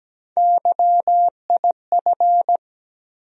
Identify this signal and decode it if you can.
Morse code, heard as a keyed tone, about 17 wpm, 700 Hz. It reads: YIF
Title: USA Call Sign Postfixes - 3 Letters 17wpm